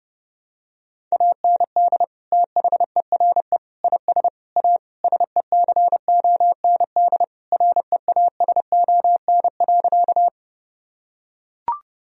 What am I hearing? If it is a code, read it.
Morse code, heard as a keyed tone, about 30 wpm, 700 Hz. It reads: AND THERE IS A SECOND REASON.
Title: Sentences from Top 1000 Words - Review Part 1 - 30wpm